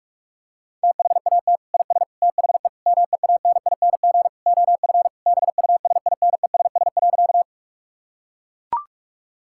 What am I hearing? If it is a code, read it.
Morse code, heard as a keyed tone, about 41 wpm, 700 Hz. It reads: THAT IS THE MEANING OF BUSINESS.
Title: Sentences from Top 1000 Words - Review Part 6 - 45wpm